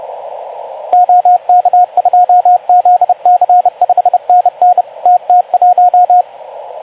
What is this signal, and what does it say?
Morse code, heard as a keyed tone, about 30 wpm, 720 Hz, with radio noise. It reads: OK2ZC 5NN TT1
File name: Archiv / CUC189 / ok2zcq cuc 189